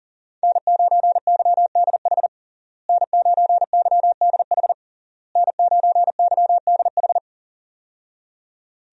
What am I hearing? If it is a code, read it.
Morse code, heard as a keyed tone, about 40 wpm, 700 Hz. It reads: N9YDH N9YDH N9YDH
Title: USA Call Signs - 5 Characters - Repeated 3x - 40wpm